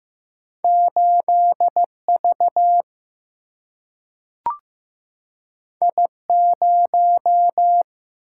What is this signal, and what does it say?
Morse code, heard as a keyed tone, about 15 wpm, 700 Hz. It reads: 8 V I 0
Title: Single-Letter Single-Number 15wpm